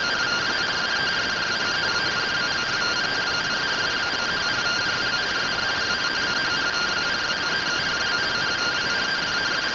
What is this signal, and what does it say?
Two-tone teleprinter signal, about 170 Hz shift.
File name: findings / MR AXIAL 33,99mm 1.gen-24 rtty rx